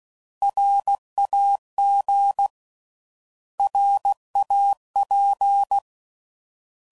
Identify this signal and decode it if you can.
Morse code, heard as a keyed tone, about 16 wpm, 800 Hz. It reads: RAG RAP
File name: Prefixes / By First Letter / Rs